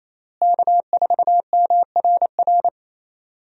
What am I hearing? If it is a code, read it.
Morse code, heard as a keyed tone, about 28 wpm, 700 Hz. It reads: K4MRR
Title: USA Call Signs - 5 Characters - Repeated 3x - 28wpm